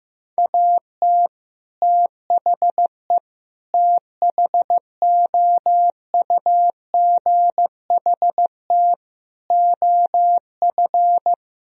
Morse code, heard as a keyed tone, about 15 wpm, 700 Hz. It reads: AT THE THOUGHT OF
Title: Sentences from Top 1000 Words - Review Part 2 - 15wpm